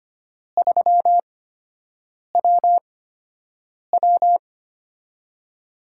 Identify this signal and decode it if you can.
Morse code, heard as a keyed tone, about 25 wpm, spaced as for 7 wpm, 700 Hz. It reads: 3WW
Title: USA Call Signs - 4 Characters - 8x Character Spacing - 25wpm